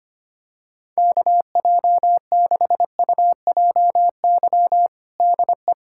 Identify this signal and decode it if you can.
Morse code, heard as a keyed tone, about 25 wpm, 700 Hz. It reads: KJ6UJY DE
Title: USA Callsign DE Callsign - 6 Characters - 25wpm